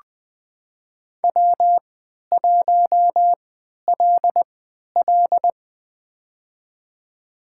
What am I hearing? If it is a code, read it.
Morse code, heard as a keyed tone, about 20 wpm, spaced as for 11 wpm, 700 Hz. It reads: W1LL
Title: USA Call Signs - 4 Characters - 3x Character Spacing - 20wpm